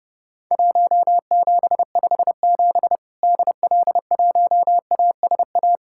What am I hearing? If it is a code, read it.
Morse code, heard as a keyed tone, about 30 wpm, 700 Hz. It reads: 1757 DL1ASA
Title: Worked All Europe DX Contest - QTC Exchange - 30wpm